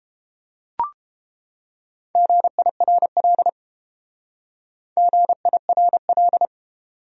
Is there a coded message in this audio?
Yes: Morse code from 2.2 to 6.5 s, heard as a keyed tone, about 31 wpm, 700 Hz. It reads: GIRL GIRL